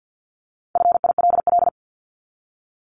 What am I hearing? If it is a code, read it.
Morse code, heard as a keyed tone, about 50 wpm, 700 Hz. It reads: FILL
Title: ED Suffix - Top 100 Words with Base Words 50wpm